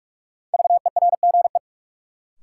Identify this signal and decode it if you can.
Morse code, heard as a keyed tone, about 44 wpm, 700 Hz. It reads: VERGE
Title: Sets of 4 Words - New Word Drill - 45wpm